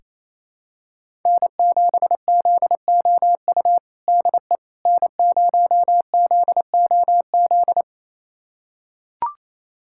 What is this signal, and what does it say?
Morse code, heard as a keyed tone, about 28 wpm, 700 Hz. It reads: N7ZOU DE N0ZOZ
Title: USA Callsign DE Callsign - 5 Characters - 28wpm